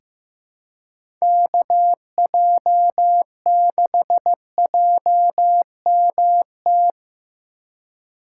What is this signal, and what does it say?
Morse code, heard as a keyed tone, about 15 wpm, 700 Hz. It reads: KJ6JMT
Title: USA Call Signs - 6 Characters - 3x Character Spacing - 15wpm